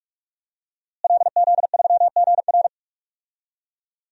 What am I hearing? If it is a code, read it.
Morse code, heard as a keyed tone, about 41 wpm, 700 Hz. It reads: LZ3GR
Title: DXCC-100 Call Signs - 5 Characters - Repeated 3x - 45wpm